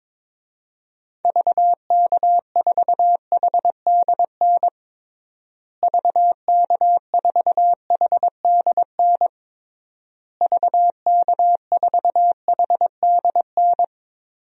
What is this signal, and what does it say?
Morse code, heard as a keyed tone, about 22 wpm, 700 Hz. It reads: VK4HDN VK4HDN VK4HDN
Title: DXCC-100 Call Signs - 6 Characters - Repeated 3x - 22wpm